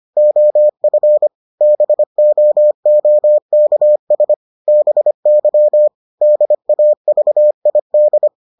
Morse code, heard as a keyed tone, about 25 wpm, 600 Hz. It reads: OF BOOKS BY DAVID